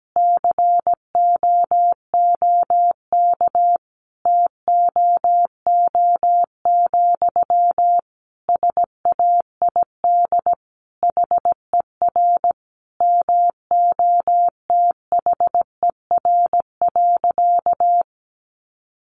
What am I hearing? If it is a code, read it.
Morse code, heard as a keyed tone, about 17 wpm, 700 Hz. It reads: COOK TOO, SAID HER MOTHER.
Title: Sentences from Top 700 Words 17wpm